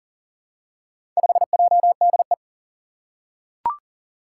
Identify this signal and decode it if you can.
Morse code, heard as a keyed tone, about 40 wpm, 700 Hz. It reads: 5JDE
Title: Four Challenging Letter-Number Combinations 40wpm